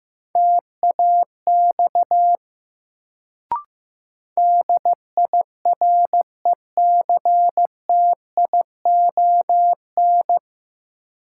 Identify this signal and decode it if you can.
Morse code, heard as a keyed tone, about 15 wpm, 700 Hz. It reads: TAX DIRECTION
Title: Sentences from Top 1000 Words - Review Part 5 - New Word Drill - 15wpm